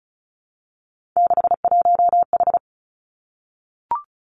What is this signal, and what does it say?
Morse code, heard as a keyed tone, about 36 wpm, 700 Hz. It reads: 61H
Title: Three Challenging Letter-Number Combinations 35wpm